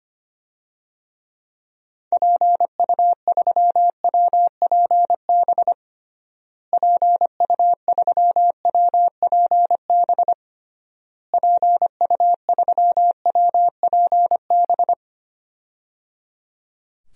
Morse code, heard as a keyed tone, about 25 wpm, 700 Hz. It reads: PU3WPB PU3WPB PU3WPB
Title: DXCC-100 Call Signs - 6 Characters - Repeated 3x - 25wpm